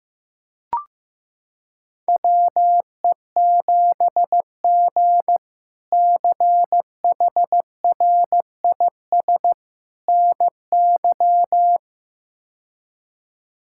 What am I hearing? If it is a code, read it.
Morse code, heard as a keyed tone, about 16 wpm, 700 Hz. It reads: WE7G CHRIS NY
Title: K1USN SST QSO Practice - 15wpm